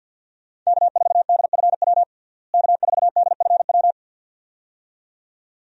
Morse code, heard as a keyed tone, about 51 wpm, 700 Hz. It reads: K4DFW K4DFW
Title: USA Call Signs - 5 Characters - Repeated 2x - 50wpm